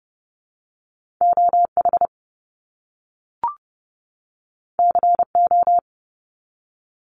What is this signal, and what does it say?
Morse code, heard as a keyed tone, about 30 wpm, 700 Hz. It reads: OH CO
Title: US States 30wpm